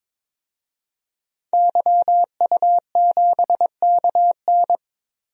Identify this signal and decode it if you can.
Morse code, heard as a keyed tone, about 22 wpm, 700 Hz. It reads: YU7KN
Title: DXCC-100 Call Signs - 5 Characters - 22wpm